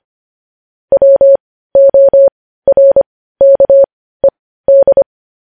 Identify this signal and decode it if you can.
Morse code, heard as a keyed tone, about 25 wpm, spaced as for 15 wpm, 550 Hz. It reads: WORKED
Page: News headlines in Morse code at 15 wpm